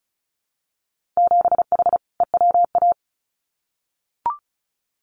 Morse code, heard as a keyed tone, about 36 wpm, 700 Hz. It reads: 7H EWA
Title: Winter Field Day Exchanges Encore 35wpm